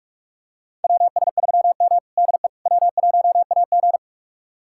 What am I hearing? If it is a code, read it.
Morse code, heard as a keyed tone, about 46 wpm, 700 Hz. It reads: WS3M DE W1AG
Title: USA Callsign DE Callsign - 4 Characters - 45wpm